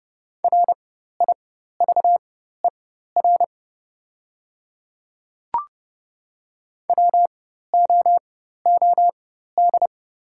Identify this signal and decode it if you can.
Morse code, heard as a keyed tone, about 31 wpm, spaced as for 14 wpm, 700 Hz. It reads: RIVER WOOD
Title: 4x Character Spacing - Top 300 Words - 30wpm